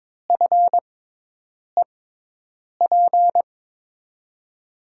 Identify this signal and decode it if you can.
Morse code, heard as a keyed tone, about 22 wpm, spaced as for 8 wpm, 700 Hz. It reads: FEP
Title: USA Call Signs - 6x Character Spacing - 22wpm